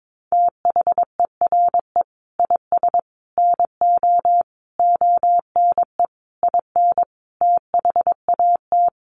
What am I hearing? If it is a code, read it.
Morse code, heard as a keyed tone, about 22 wpm, 700 Hz. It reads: THERE IS NO ONE IN THAT
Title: Sentences from Top 500 Words 22wpm